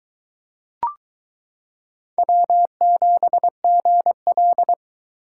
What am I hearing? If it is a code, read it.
Morse code, heard as a keyed tone, about 22 wpm, 700 Hz. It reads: W7GL